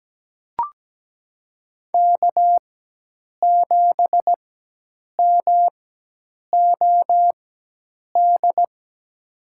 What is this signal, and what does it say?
Morse code, heard as a keyed tone, about 17 wpm, spaced as for 8 wpm, 700 Hz. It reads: K7MOD
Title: USA Call Signs - 4x Character Spacing - 17wpm